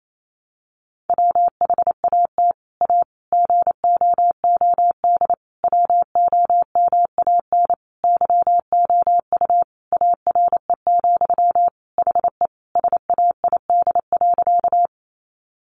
Morse code, heard as a keyed tone, about 28 wpm, 700 Hz. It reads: WHAT A GOOD WOMAN YOU ARE, HE SAID.
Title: Sentences from Top 1100 Words 28wpm